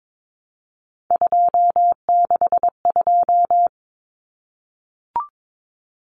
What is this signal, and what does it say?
Morse code, heard as a keyed tone, about 22 wpm, 700 Hz. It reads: 262
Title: Numbers - 3-Digit 22wpm